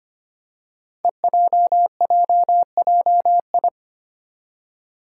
Morse code, heard as a keyed tone, about 25 wpm, 700 Hz. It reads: EJJJI
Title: Five Challenging Letter-Number Combinations 25wpm